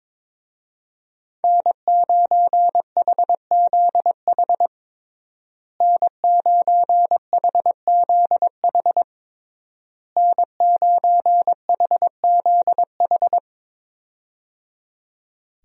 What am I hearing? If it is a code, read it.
Morse code, heard as a keyed tone, about 22 wpm, 700 Hz. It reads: N9HZH N9HZH N9HZH
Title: USA Call Signs - 5 Characters - Repeated 3x - 22wpm